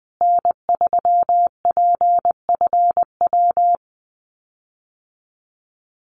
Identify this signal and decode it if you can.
Morse code, heard as a keyed tone, about 20 wpm, 700 Hz. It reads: N3PFW
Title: USA Call Signs Repeated 3x 20wpm